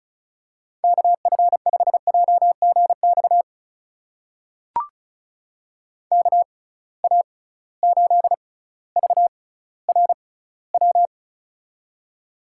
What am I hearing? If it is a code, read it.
Morse code, heard as a keyed tone, about 34 wpm, spaced as for 18 wpm, 700 Hz. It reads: KF5JGX KA8VRW
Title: USA Call Signs - 6 Characters - 6x Character Spacing - 35wpm